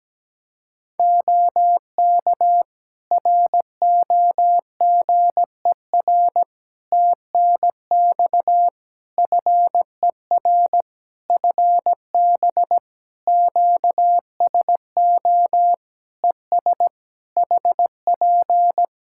Morse code, heard as a keyed tone, about 17 wpm, 700 Hz. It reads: OK ROGER TNX FER FB QSO ES HP
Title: Practice QSO - Exchange 4 - Ending - 17wpm